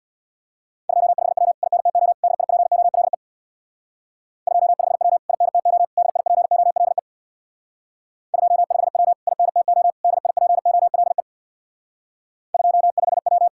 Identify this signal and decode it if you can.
Morse code, heard as a keyed tone, about 66 wpm, 700 Hz. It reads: 25W INTO DIPOLE 25W INTO DIPOLE 25W INTO DIPOLE 25W